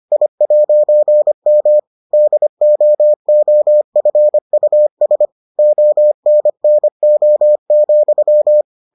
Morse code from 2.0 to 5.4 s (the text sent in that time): DOOFUS